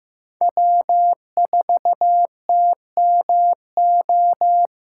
Morse code, heard as a keyed tone, about 15 wpm, 700 Hz. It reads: W4TMO
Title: K1USN SST QSO Practice - 15wpm